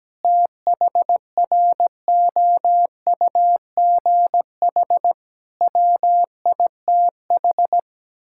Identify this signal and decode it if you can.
Morse code, heard as a keyed tone, about 17 wpm, 700 Hz. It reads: THROUGH WITH
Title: Sentences from Top 1000 Words - Review Part 6 - 17wpm